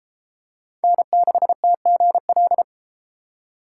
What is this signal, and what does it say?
Morse code, heard as a keyed tone, about 32 wpm, 700 Hz. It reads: N6TGL